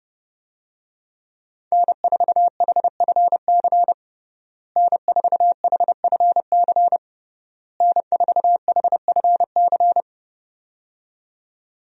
Morse code, heard as a keyed tone, about 30 wpm, 700 Hz. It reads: N4HFC N4HFC N4HFC
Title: USA Call Signs - 5 Characters - Repeated 3x - 30wpm